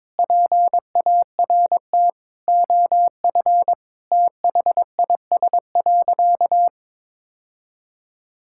Morse code, heard as a keyed tone, about 22 wpm, 700 Hz. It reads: PART OF THIS.
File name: practice / 22wpm / Sentences with 3-Word Phrases - 22wpm